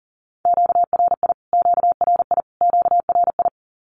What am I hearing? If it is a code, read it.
Morse code, heard as a keyed tone, about 40 wpm, 700 Hz. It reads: QRI QRI QRI
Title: Q-Codes Repeated 3x 40wpm